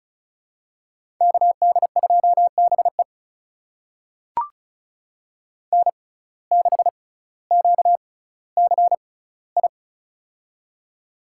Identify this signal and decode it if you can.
Morse code, heard as a keyed tone, about 36 wpm, spaced as for 18 wpm, 700 Hz. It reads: KD2BE N6QCI
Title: USA Call Signs - 5 Characters - 6x Character Spacing - 35wpm